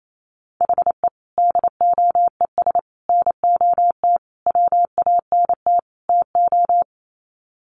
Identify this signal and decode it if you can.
Morse code, heard as a keyed tone, about 28 wpm, 700 Hz. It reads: HE DOES NOT WANT TO
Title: Sets of 5 Words 28wpm